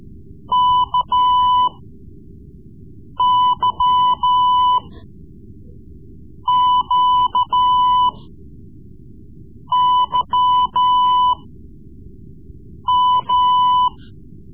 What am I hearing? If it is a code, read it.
Morse code, heard as a keyed tone, about 14 wpm, spaced as for 5 wpm, 1000 Hz. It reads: KYQYM
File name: Morse Lessons / Morse Lessons / Tape-3B